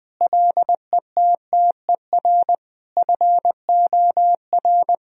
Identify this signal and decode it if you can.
Morse code, heard as a keyed tone, about 20 wpm, 700 Hz. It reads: LETTER FOR
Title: Sentences from Top 1000 Words - Review Part 2 - 20wpm